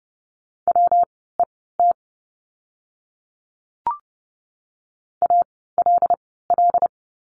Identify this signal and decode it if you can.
Morse code, heard as a keyed tone, about 30 wpm, spaced as for 17 wpm, 700 Hz. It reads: WET ALL
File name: course / 30:10wpm / 115 Lesson 19f - Letter w - Words 30wpm